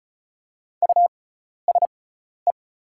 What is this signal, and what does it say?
Morse code, heard as a keyed tone, about 34 wpm, spaced as for 12 wpm, 700 Hz. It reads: USE
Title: 6x Character Spacing - Top 100 Words - 35wpm